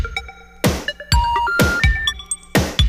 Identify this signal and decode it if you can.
Special information tone, with noise.